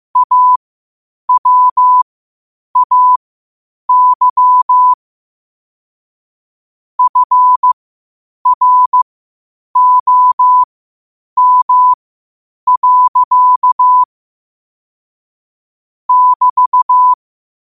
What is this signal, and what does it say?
Morse code, heard as a keyed tone, about 15 wpm, spaced as for 8 wpm, 1000 Hz. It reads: AWAY FROM. <BT>
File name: qotdfiles / 8wpm